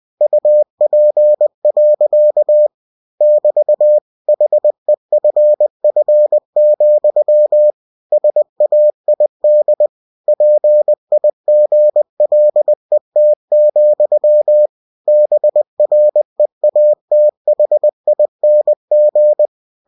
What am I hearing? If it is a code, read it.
Morse code, heard as a keyed tone, about 20 wpm, 600 Hz. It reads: UP. <BT> HEFF, SAID PIGLET, BREATHING